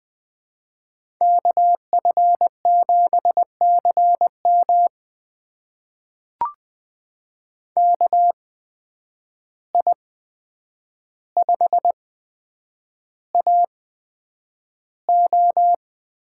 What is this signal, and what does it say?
Morse code, heard as a keyed tone, about 20 wpm, spaced as for 9 wpm, 700 Hz. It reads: KF7CM KI5AO
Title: USA Call Signs - 8x Character Spacing - 20wpm